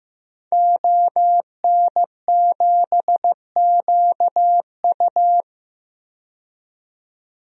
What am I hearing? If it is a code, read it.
Morse code, heard as a keyed tone, about 15 wpm, 700 Hz. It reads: ON7QU
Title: DXCC-100 Call Signs - 5 Characters - 15wpm